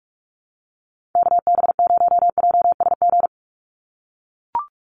Morse code, heard as a keyed tone, about 44 wpm, 700 Hz. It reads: KB0JSG
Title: USA Call Signs 45wpm